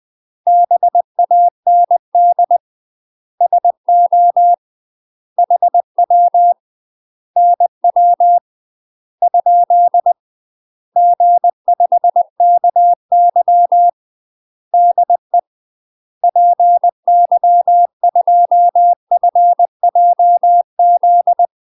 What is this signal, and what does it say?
Morse code, heard as a keyed tone, about 20 wpm, 700 Hz. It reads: BAND SO HW NW ? G5KY DE PY2FJZ
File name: UK MASTER QSOs TRAINING / 20WPM / UK MASTER QSOs 0037